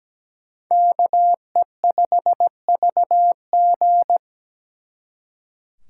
Morse code, heard as a keyed tone, about 21 wpm, 700 Hz. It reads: KE5VG
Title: Speed Racing - USA 2x2 Call Signs 17wpm